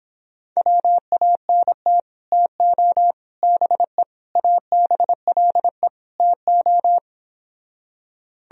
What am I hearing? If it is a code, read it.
Morse code, heard as a keyed tone, about 26 wpm, 700 Hz. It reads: WANT TO BE ABLE TO